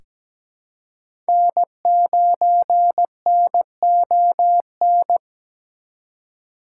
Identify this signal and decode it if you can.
Morse code, heard as a keyed tone, about 17 wpm, 700 Hz. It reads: N9NON
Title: USA Call Signs - 3x Character Spacing - 17wpm